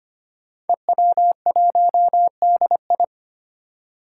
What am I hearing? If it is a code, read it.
Morse code, heard as a keyed tone, about 25 wpm, 700 Hz. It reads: EW1DI
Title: Five Challenging Letter-Number Combinations 25wpm